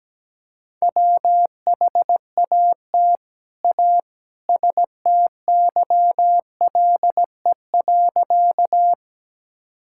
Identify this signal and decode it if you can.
Morse code, heard as a keyed tone, about 17 wpm, 700 Hz. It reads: WHAT A STYLE.
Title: Sentences from Top 1000 Words - Review Part 5 - 17wpm